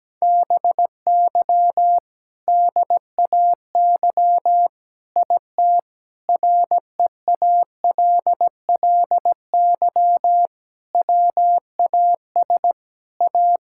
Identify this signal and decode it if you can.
Morse code, heard as a keyed tone, about 17 wpm, 700 Hz. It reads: BY DAY IT REALLY WAS A
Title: Sentences from Top 1600 Words 17wpm